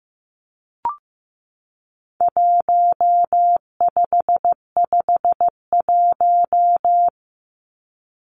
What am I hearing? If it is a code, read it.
Morse code, heard as a keyed tone, about 15 wpm, 700 Hz. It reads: 1551
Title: Numbers - Contest Style - 4-Digit 15wpm